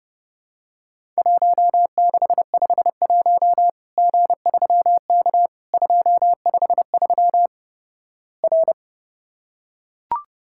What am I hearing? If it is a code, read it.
Morse code, heard as a keyed tone, about 32 wpm, 700 Hz. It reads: 1651 G3K 253 R
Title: Worked All Europe DX Contest - QTC Exchange - 30wpm